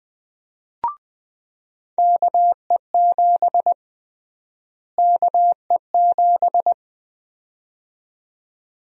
Morse code, heard as a keyed tone, about 21 wpm, 700 Hz. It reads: KE7 KE7
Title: USA Call Sign Prefixes - Repeated 2x - 20wpm